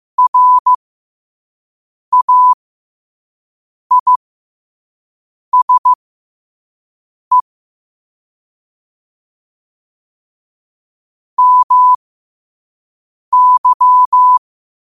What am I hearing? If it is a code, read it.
Morse code, heard as a keyed tone, about 15 wpm, spaced as for 5 wpm, 1000 Hz. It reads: RAISE MY
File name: qotdfiles / 5wpm